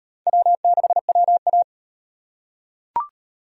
Morse code, heard as a keyed tone, about 38 wpm, 700 Hz. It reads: W6WA